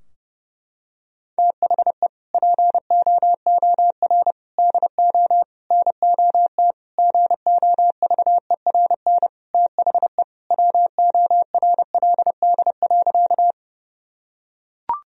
Morse code, heard as a keyed tone, about 30 wpm, 700 Hz. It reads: THE POOR DO NOT GOVERN THE WORLD.
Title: Sentences from Top 1000 Words - Review Part 2 - 30wpm